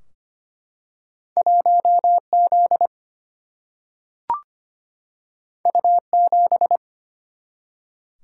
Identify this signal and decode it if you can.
Morse code, heard as a keyed tone, about 25 wpm, 700 Hz. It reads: 1Z U7
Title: Single-Letter Single-Number 25wpm